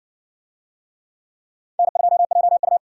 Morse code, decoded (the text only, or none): N2JU